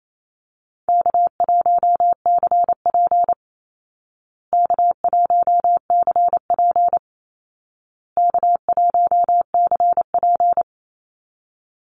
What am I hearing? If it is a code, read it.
Morse code, heard as a keyed tone, about 28 wpm, 700 Hz. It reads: K1CP K1CP K1CP
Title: Call Signs Repeated 2x - 28wpm